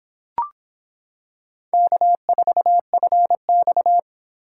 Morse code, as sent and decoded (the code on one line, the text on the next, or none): -.- ....- ..-. -..-
K4FX